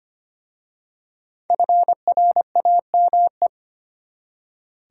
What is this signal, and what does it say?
Morse code, heard as a keyed tone, about 25 wpm, 700 Hz. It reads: FRAME
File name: practice / 25wpm / Easy Words with Hints - 25wpm